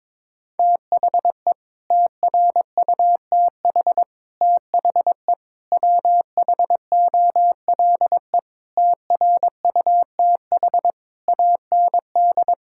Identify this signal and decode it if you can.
Morse code, heard as a keyed tone, about 22 wpm, 700 Hz. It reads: THE TRUTH THE WHOLE TRUTH AND
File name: Sets of 9 Words - 22wpm